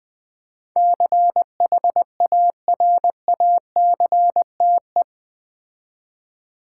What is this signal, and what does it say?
Morse code, heard as a keyed tone, about 20 wpm, 700 Hz. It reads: CHARACTE
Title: Build A Word - 4 Letters At A Time - 12-Letter Words - 20wpm